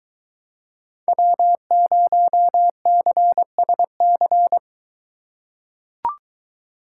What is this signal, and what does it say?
Morse code, heard as a keyed tone, about 22 wpm, 700 Hz. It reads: W0CSC